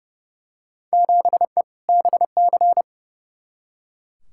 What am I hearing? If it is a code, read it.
Morse code, heard as a keyed tone, about 32 wpm, 700 Hz. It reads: 7E BC
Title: Field Day Exchanges Encore - 30wpm